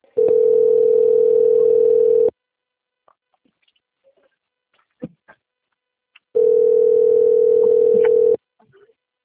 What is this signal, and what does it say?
Ringback tone (North America), 440 + 480 Hz.